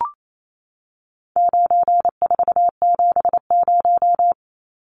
Morse code, heard as a keyed tone, about 28 wpm, 700 Hz. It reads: 9470
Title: Numbers - 4-Digit 28wpm